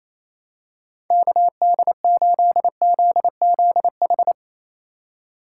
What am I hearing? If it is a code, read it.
Morse code, heard as a keyed tone, about 28 wpm, 700 Hz. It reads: KD8ZZH
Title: USA Call Signs - 6 Characters - 2x Character Spacing - 28wpm